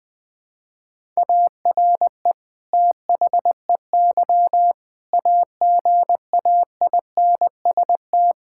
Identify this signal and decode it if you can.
Morse code, heard as a keyed tone, about 20 wpm, 700 Hz. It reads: ARE THEY AGAINST
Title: Sentences from Top 1000 Words - Review Part 5 - 20wpm